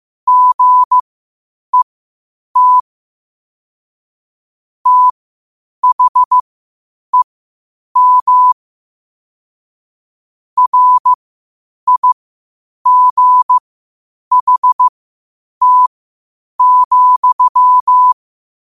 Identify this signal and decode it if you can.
Morse code, heard as a keyed tone, about 15 wpm, spaced as for 8 wpm, 1000 Hz. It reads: GET THEM RIGHT,
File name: qotdfiles / 8wpm